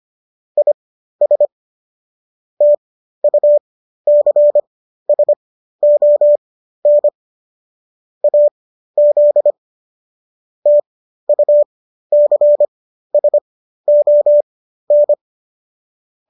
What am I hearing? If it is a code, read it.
Morse code, heard as a keyed tone, about 25 wpm, spaced as for 13 wpm, 590 Hz. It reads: IS TUCSON AZ TUCSON